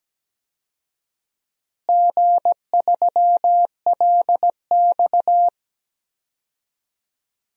Morse code, heard as a keyed tone, about 17 wpm, 700 Hz. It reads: G3LX
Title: DXCC-100 Call Signs - 4 Characters - Repeated 3x - 17wpm